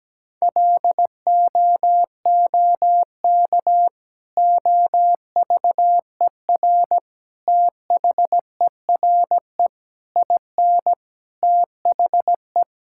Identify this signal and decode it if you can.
Morse code, heard as a keyed tone, about 17 wpm, 700 Hz. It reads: LOOK OVER THERE IN THE
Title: Sentences from Top 1600 Words 17wpm